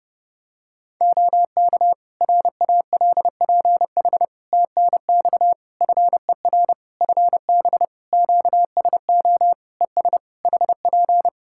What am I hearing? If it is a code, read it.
Morse code, heard as a keyed tone, about 30 wpm, 700 Hz. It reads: OK RALPH TNX FER FB QSO ES HP
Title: Practice QSO - Exchange 4 - Ending - 30wpm